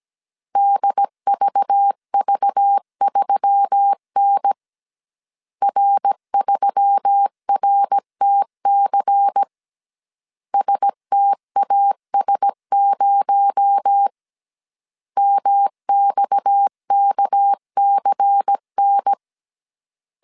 Morse code, heard as a keyed tone, about 17 wpm, 800 Hz. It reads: DVV3N R3RTC STAS0 MXKCN
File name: CW / DET / 1 9 1500C S A 16wpm 2828